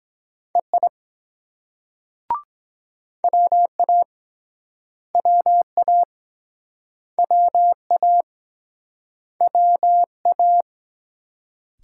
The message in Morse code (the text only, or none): EI WA WA WA WA